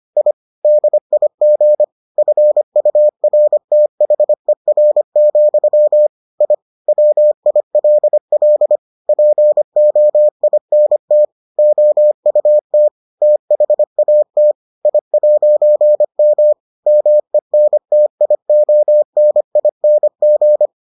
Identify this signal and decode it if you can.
Morse code, heard as a keyed tone, about 25 wpm, 600 Hz. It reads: I DIG FURTHER, I WILL POINT OUT THAT I'M MENTIONING